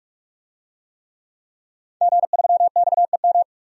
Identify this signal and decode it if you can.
Morse code, heard as a keyed tone, about 44 wpm, 700 Hz. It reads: G3XEM
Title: DXCC-100 Call Signs - 5 Characters - 45wpm